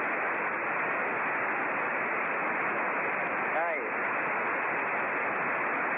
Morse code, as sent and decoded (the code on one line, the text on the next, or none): none